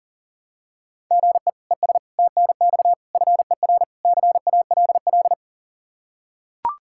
Morse code, heard as a keyed tone, about 40 wpm, 700 Hz. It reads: GE ES TNX FER CALL
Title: K1USN SST QSO Practice 40wpm